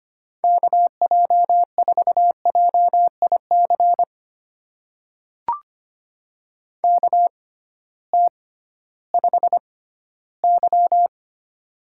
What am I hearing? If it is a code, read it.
Morse code, heard as a keyed tone, about 25 wpm, spaced as for 14 wpm, 700 Hz. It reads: KJ4JIC KT5Y
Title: USA Call Signs - 6x Character Spacing - 25wpm